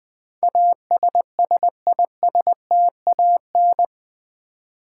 Morse code, heard as a keyed tone, about 20 wpm, 700 Hz. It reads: ASSISTAN
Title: Build A Word - 4 Letters At A Time - 10-Letter Words - 20wpm